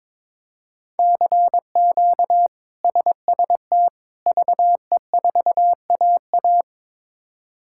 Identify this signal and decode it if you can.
Morse code, heard as a keyed tone, about 22 wpm, 700 Hz. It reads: CQ SST VE4AA
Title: K1USN SST QSO Practice - 22wpm